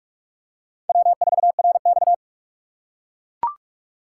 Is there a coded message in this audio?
Yes: Morse code from 0.9 to 2.1 s, heard as a keyed tone, about 39 wpm, 700 Hz. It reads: W4RX